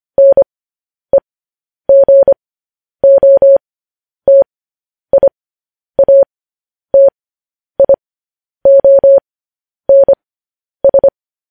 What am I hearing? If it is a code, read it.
Morse code, heard as a keyed tone, about 25 wpm, spaced as for 10 wpm, 550 Hz. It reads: NEGOTIATIONS
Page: news-headlines-in-morse-code-at-10-wpm